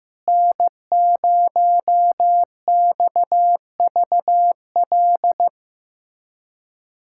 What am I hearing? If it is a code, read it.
Morse code, heard as a keyed tone, about 16 wpm, 700 Hz. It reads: N0XVL
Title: USA Callsign DE Callsign 15wpm